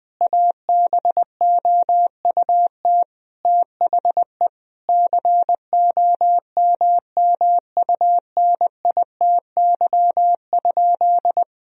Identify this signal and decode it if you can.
Morse code, heard as a keyed tone, about 20 wpm, 700 Hz. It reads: ABOUT THE COMMUNITY?
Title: Sentences from Top 1000 Words - Review Part 4 - 20wpm